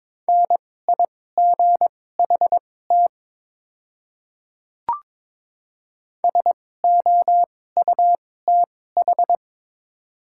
Morse code, heard as a keyed tone, about 22 wpm, spaced as for 16 wpm, 700 Hz. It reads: NIGHT SOUTH
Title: 2x Character Spacing - Top 500 Words - 22wpm